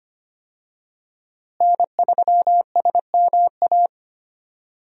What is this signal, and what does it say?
Morse code, heard as a keyed tone, about 25 wpm, 700 Hz. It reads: N3SMA
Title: USA Call Signs - 5 Characters - Repeated 2x - 25wpm